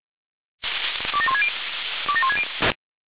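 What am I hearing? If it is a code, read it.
Five-tone selective call (ZVEI), heard as quick runs of five tones, tones of about 80 ms. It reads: 27160 27160
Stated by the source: ZVEI five-tone selective call 27002 received on SDR.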